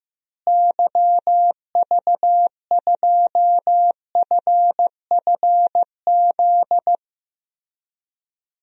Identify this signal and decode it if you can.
Morse code, heard as a keyed tone, about 15 wpm, 700 Hz. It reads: YV2FFZ
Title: DXCC-100 Call Signs - 6 Characters - Repeated 3x - 15wpm